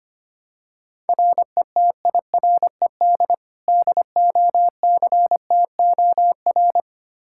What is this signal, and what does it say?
Morse code, heard as a keyed tone, about 25 wpm, 700 Hz. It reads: RETIRED DOCTOR
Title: Practice QSO - Exchange 3 - More Info - 25wpm